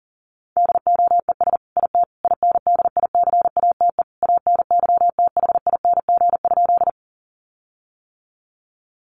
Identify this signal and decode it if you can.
Morse code, heard as a keyed tone, about 40 wpm, 700 Hz. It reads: DOES IT INDICATE ANYTHING?
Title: Sentences from Top 1300 Words 40wpm